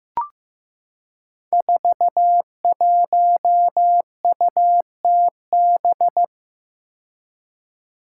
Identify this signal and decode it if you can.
Morse code, heard as a keyed tone, about 15 wpm, 700 Hz. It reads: 41UTB
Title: Five Challenging Letter-Number Combinations 15wpm